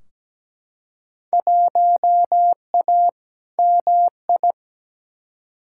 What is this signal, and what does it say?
Morse code, heard as a keyed tone, about 17 wpm, 700 Hz. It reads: 1A MI
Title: Field Day Exchanges - 17wpm